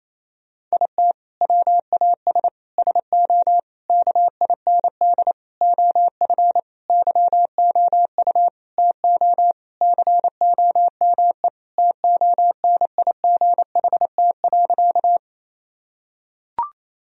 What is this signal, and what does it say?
Morse code, heard as a keyed tone, about 27 wpm, 700 Hz. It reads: IT WAS SO KIND OF YOU TO COME TONIGHT.
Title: Sentences from Top 1000 Words - Review Part 3 - 28wpm